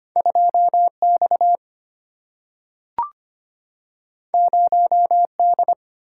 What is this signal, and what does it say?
Morse code, heard as a keyed tone, about 25 wpm, 690 Hz. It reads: 2X 0D
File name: Two Letter-Number Combinations - 25wpm